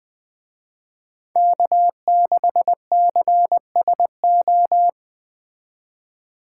Morse code, heard as a keyed tone, about 24 wpm, 700 Hz. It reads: K6CSO
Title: Speed Racing - USA 1x3 Call Signs 20wpm